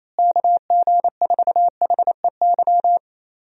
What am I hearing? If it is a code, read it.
Morse code, heard as a keyed tone, about 28 wpm, 700 Hz. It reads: KG4HEY
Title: USA Call Signs - 6 Characters - Repeated 3x - 28wpm